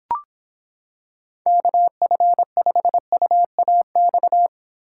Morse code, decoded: KF5UAX

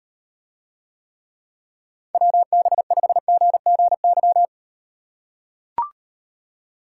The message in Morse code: WB5GGY

